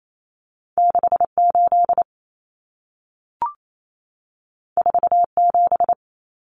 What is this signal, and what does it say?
Morse code, heard as a keyed tone, about 28 wpm, 700 Hz. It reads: 68 47
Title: Numbers - 2-Digit 28wpm